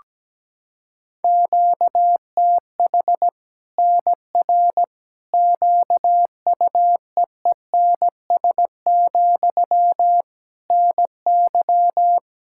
Morse code, heard as a keyed tone, about 17 wpm, 700 Hz. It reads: QTH NR QUEENS, NY
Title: Practice QSO - Exchange 1 - Beginning - 17wpm